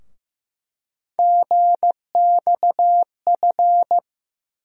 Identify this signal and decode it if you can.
Morse code, heard as a keyed tone, about 15 wpm, 700 Hz. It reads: GXF
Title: USA Call Sign Postfixes - 3 Letters 15wpm